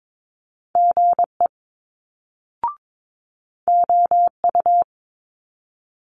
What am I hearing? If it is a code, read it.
Morse code, heard as a keyed tone, about 22 wpm, 700 Hz. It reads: GE OU
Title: Multi-letter Phonemes Quick Fire 22wpm